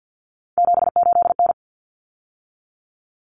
Morse code, heard as a keyed tone, about 50 wpm, 700 Hz. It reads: 78N
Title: Common Cut Numbers - 3-Digit 50wpm